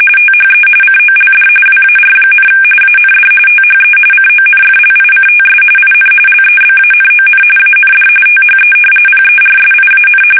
RTTY, about 45.45 baud. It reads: THE COLD WAR, AND THE CONFLICTS IN LEBANON AND THE PERSIAN
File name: fsk850 / battleshipnj